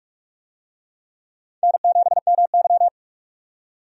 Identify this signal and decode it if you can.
Morse code, heard as a keyed tone, about 46 wpm, 700 Hz. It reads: N7MY
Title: USA Call Signs - 4 Characters - 45wpm